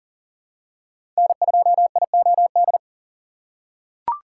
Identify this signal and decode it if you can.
Morse code, heard as a keyed tone, about 40 wpm, 700 Hz. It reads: N2IOD
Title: USA Call Signs - 5 Characters - 2x Character Spacing - 40wpm